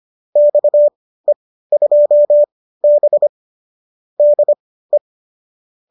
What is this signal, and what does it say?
Morse code, heard as a keyed tone, about 25 wpm, spaced as for 15 wpm, 570 Hz. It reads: XE2B DE
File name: qso204 15